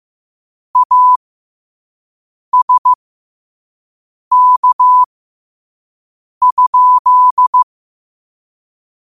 Morse code, heard as a keyed tone, about 15 wpm, spaced as for 5 wpm, 1000 Hz. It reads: ASK?
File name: qotdfiles / 5wpm